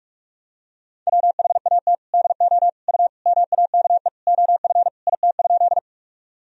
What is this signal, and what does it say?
Morse code, heard as a keyed tone, about 44 wpm, 700 Hz. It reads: WHAT DO U MAKE OF IT?
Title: Sentences from Top 100 Words and Abbreviations 45wpm